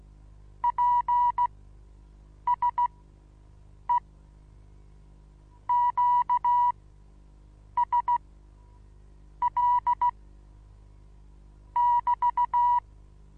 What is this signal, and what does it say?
Morse code, heard as a keyed tone, about 15 wpm, spaced as for 6 wpm, 970 Hz. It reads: PSE QSL <BT>
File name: Onlinelehrgang / morsen / DJ4UF Morsen25